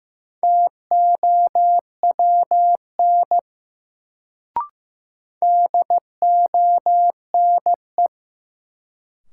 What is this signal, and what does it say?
Morse code, heard as a keyed tone, about 15 wpm, 700 Hz. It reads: TOWN DONE
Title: Sentences from Top 500 Words - New Word Drill - 15wpm